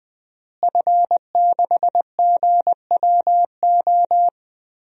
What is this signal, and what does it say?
Morse code, heard as a keyed tone, about 21 wpm, 700 Hz. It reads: F6GWO
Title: DXCC-100 Call Signs - 5 Characters - Repeated 3x - 20wpm